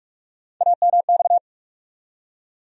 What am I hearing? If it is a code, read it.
Morse code, heard as a keyed tone, about 46 wpm, 700 Hz. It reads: AMX
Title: USA Call Sign Postfixes - 3 Letters 45wpm